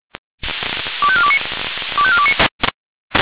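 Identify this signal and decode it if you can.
Five-tone selective call (ZVEI), heard as quick runs of five tones, tones of about 70 ms. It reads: 25429 25429
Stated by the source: ZVEI five-tone selective call received on an SDR.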